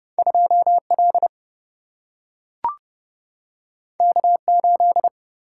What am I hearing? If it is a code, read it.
Morse code, heard as a keyed tone, about 30 wpm, 700 Hz. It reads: 2 L K 8
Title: Single-Letter Single-Number 30wpm